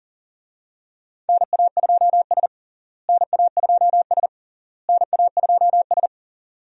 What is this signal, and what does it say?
Morse code, heard as a keyed tone, about 40 wpm, 700 Hz. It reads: NA2S NA2S NA2S
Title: USA Call Signs - 4 Characters - Repeated 3x - 40wpm